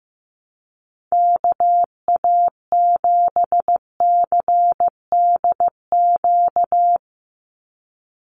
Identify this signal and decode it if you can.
Morse code, heard as a keyed tone, about 15 wpm, 700 Hz. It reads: KA7CDQ
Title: USA Call Signs 15wpm